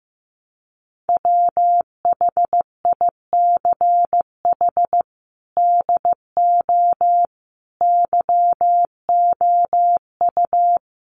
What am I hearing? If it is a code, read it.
Morse code, heard as a keyed tone, about 15 wpm, 700 Hz. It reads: WHICH DO YOU
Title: Sentences from Top 1100 Words 15wpm